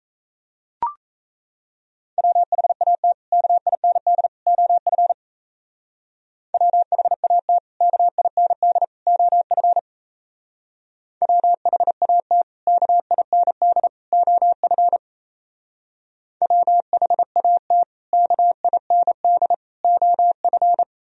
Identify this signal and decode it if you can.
Morse code, heard as a keyed tone, about 35 wpm, 700 Hz. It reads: WHAT KIND OF WHAT KIND OF WHAT KIND OF WHAT KIND OF
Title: Speed-Racing - Sets of 3 Words 28wpm